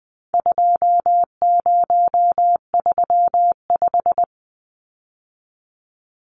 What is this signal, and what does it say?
Morse code, heard as a keyed tone, about 20 wpm, 700 Hz. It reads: 2035
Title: Numbers - 4-Digit 20wpm